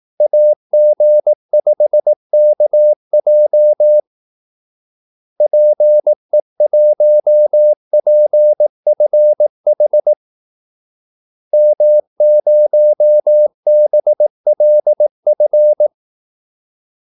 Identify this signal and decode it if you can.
Morse code, heard as a keyed tone, about 18 wpm, 600 Hz. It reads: AG5KJ PE1PFH M0BLF
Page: Kaldesignaler 18 wpm